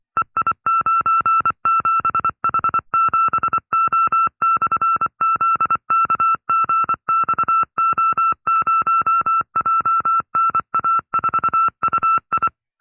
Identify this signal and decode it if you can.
Morse code, heard as a keyed tone, about 24 wpm, 1340 Hz. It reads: EI97H7O/ZKGXO0JNA4UI
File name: FreeFiles / CW-practice / 20WPM-side2